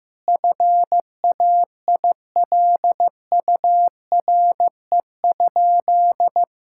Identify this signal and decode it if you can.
Morse code, heard as a keyed tone, about 15 wpm, 700 Hz. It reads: FAILURE?
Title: Sentences from Top 1000 Words - Review Part 6 - 15wpm